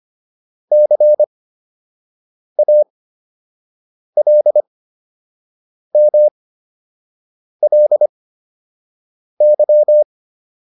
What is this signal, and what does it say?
Morse code, heard as a keyed tone, about 25 wpm, spaced as for 6 wpm, 600 Hz. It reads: CALMLY